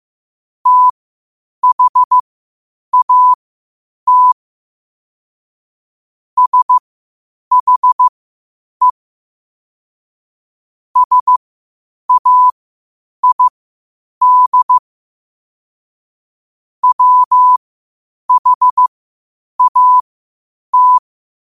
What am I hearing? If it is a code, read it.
Morse code, heard as a keyed tone, about 15 wpm, spaced as for 8 wpm, 1000 Hz. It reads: THAT SHE SAID WHAT
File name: qotdfiles / 8wpm